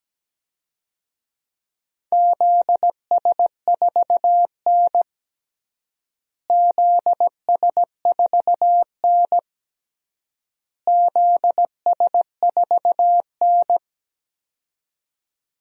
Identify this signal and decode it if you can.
Morse code, heard as a keyed tone, about 17 wpm, 700 Hz. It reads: ZS4N ZS4N ZS4N
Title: DXCC-100 Call Signs - 4 Characters - Repeated 3x - 17wpm